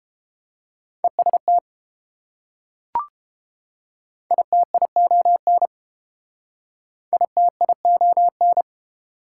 Morse code, heard as a keyed tone, about 30 wpm, 700 Hz. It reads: EST ITION ITION